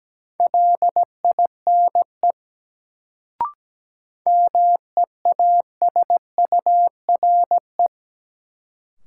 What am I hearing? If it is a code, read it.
Morse code, heard as a keyed tone, about 17 wpm, 700 Hz. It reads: LINE MEASURE
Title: Sentences from Top 1200 Words - New Word Drill Alt - 17wpm